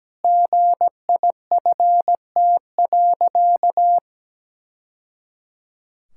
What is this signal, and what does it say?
Morse code, heard as a keyed tone, about 17 wpm, 700 Hz. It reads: GIFT.
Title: Sentences from Top 1000 Words - Review Part 6 - 17wpm